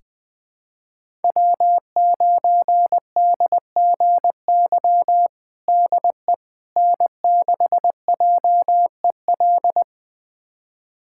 Morse code, heard as a keyed tone, about 20 wpm, 700 Hz. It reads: W9DGY DE N6JEL
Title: USA Callsign DE Callsign - 5 Characters - 20wpm